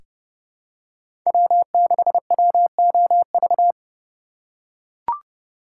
Morse code, heard as a keyed tone, about 32 wpm, 700 Hz. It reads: W6WOV
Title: USA Call Signs - 5 Characters - 2x Character Spacing - 30wpm